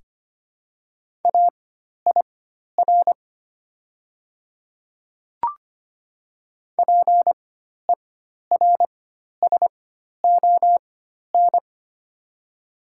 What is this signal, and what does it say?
Morse code, heard as a keyed tone, about 25 wpm, spaced as for 12 wpm, 690 Hz. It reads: AIR PERSON
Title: 4x Character Spacing - Top 500 Words - 25wpm